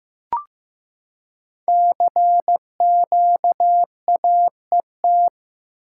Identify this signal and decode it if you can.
Morse code, heard as a keyed tone, about 15 wpm, 700 Hz. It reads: CQAET
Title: Code Groups - 5 Letter:Numbers - 15wpm